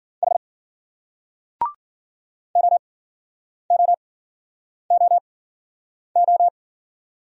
Morse code, heard as a keyed tone, about 49 wpm, spaced as for 9 wpm, 700 Hz. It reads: H OOOO
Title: Speed-Racing Single Letters 40wpm